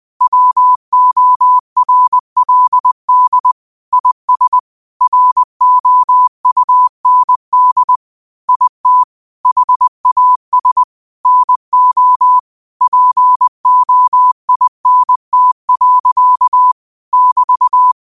Morse code, heard as a keyed tone, about 20 wpm, 1000 Hz. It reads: WORLD IS ROUND IT HAS NO POINT. <BT>
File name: qotdfiles / 20wpm